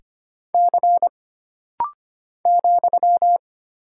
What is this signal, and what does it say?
Morse code, heard as a keyed tone, about 25 wpm, 700 Hz. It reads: C ,
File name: Single Character (Letter:Number:Punctuation:Prosign) - Rapid-Fire - 25wpm